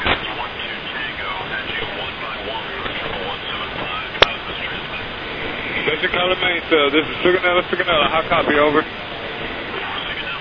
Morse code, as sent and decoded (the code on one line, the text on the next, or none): none